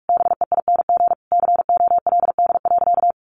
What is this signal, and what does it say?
Morse code, heard as a keyed tone, about 44 wpm, 700 Hz. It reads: BEING COLD.
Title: Sentences from Top 800 Words 45wpm